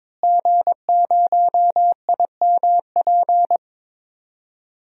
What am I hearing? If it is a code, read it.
Morse code, heard as a keyed tone, about 22 wpm, 700 Hz. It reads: G0IMP
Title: DXCC-100 Call Signs - 5 Characters - Repeated 2x - 22wpm